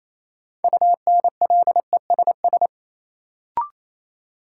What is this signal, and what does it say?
Morse code, heard as a keyed tone, about 28 wpm, 700 Hz. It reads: UNLESS
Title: Sentences from Top 1000 Words - Review Part 5 - New Word Drill - 28wpm